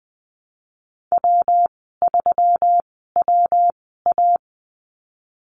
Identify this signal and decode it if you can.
Morse code, heard as a keyed tone, about 19 wpm, spaced as for 14 wpm, 700 Hz. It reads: W3WA
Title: 167 Lesson 27g - Number 7 - Call Signs 20wpm